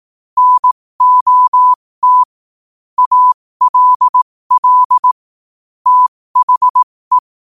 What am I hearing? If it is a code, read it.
Morse code, heard as a keyed tone, about 18 wpm, spaced as for 15 wpm, 1000 Hz. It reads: NOT ALL THE
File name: qotdfiles / 15wpm